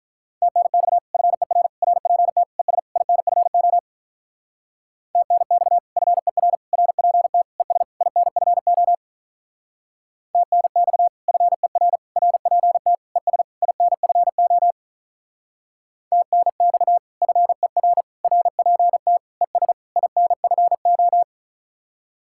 Morse code, heard as a keyed tone, about 45 wpm, 700 Hz. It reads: TNX FER RPT ES INFO TNX FER RPT ES INFO TNX FER RPT ES INFO TNX FER RPT ES INFO